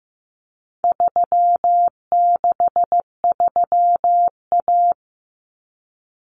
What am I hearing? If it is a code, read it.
Morse code, heard as a keyed tone, about 15 wpm, 700 Hz. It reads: 363A
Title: Common Cut Numbers - 4-Digit 15wpm